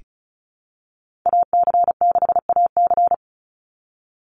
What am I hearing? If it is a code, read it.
Morse code, heard as a keyed tone, about 34 wpm, 700 Hz. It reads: AC6AC
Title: Call Signs Repeated 3x - 35wpm